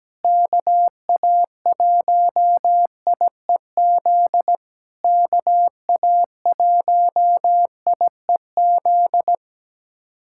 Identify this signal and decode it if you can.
Morse code, heard as a keyed tone, about 17 wpm, 700 Hz. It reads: KA1IEZ KA1IEZ
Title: SOTA QSO Practice 17wpm